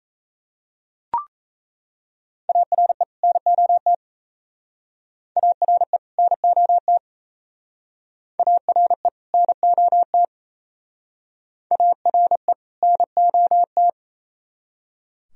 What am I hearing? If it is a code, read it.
Morse code, heard as a keyed tone, about 38 wpm, 700 Hz. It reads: ARE NOT ARE NOT ARE NOT ARE NOT